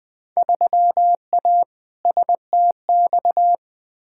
Morse code, heard as a keyed tone, about 20 wpm, 700 Hz. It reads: 3A STX
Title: Field Day Exchanges - 20wpm